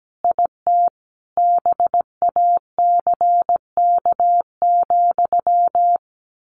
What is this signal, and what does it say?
Morse code, heard as a keyed tone, about 17 wpm, 700 Hz. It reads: IT BACK,
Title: A Sound Of Thunder 17wpm